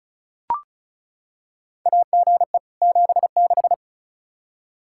Morse code, heard as a keyed tone, about 34 wpm, 700 Hz. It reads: AGE 76
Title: Practice QSO - Exchange 3 - More Info - 35wpm